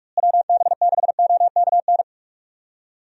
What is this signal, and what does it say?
Morse code, heard as a keyed tone, about 46 wpm, 700 Hz. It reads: WB6OKN
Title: USA Call Signs - 6 Characters - Repeated 3x - 45wpm